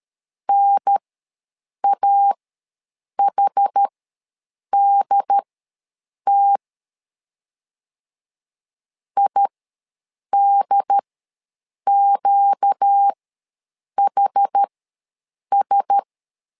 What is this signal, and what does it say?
Morse code, heard as a keyed tone, about 13 wpm, spaced as for 7 wpm, 800 Hz. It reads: NAHDT IDQHS
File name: LEZIONI / 4 Q1S ws 12wpm 1827